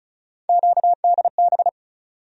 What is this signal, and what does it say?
Morse code, heard as a keyed tone, about 36 wpm, 700 Hz. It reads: QDB
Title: USA Call Sign Postfixes - 3 Letters 35wpm